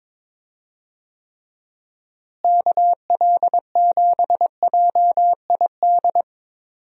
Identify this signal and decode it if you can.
Morse code, heard as a keyed tone, about 22 wpm, 700 Hz. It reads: KL7JID
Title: DXCC-100 Call Signs - 6 Characters - Repeated 2x - 22wpm